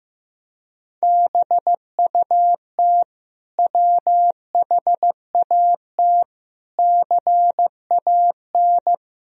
Morse code, heard as a keyed tone, about 16 wpm, 700 Hz. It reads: BUT WHAT CAN
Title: Sentences from Top 1000 Words - Review Part 4 - 15wpm